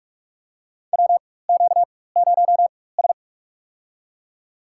Morse code, heard as a keyed tone, about 47 wpm, spaced as for 21 wpm, 700 Hz. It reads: WQ0S
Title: USA Call Signs - 4 Characters - 4x Character Spacing - 45wpm